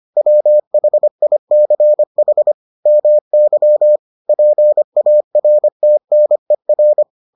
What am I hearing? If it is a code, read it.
Morse code, heard as a keyed tone, about 25 wpm, 600 Hz. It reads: WHICH MY PARTNER